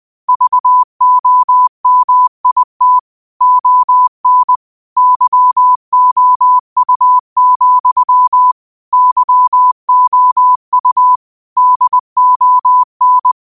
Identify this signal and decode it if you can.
Morse code, heard as a keyed tone, about 20 wpm, 1000 Hz. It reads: VOMIT ON YOU, YOU DON